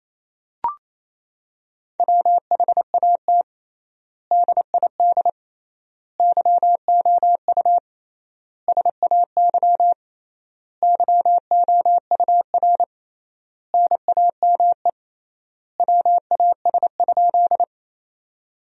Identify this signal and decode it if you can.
Morse code, heard as a keyed tone, about 28 wpm, 700 Hz. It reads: WHAT DID YOU SAY YOUR NAME WAS?
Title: 3x Word-Spacing - Sentences from Top 200 Words - 28wpm